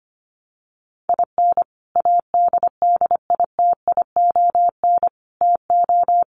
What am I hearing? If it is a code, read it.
Morse code, heard as a keyed tone, about 25 wpm, 700 Hz. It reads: IN ADDITION TO
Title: Sets of 4 Words 25wpm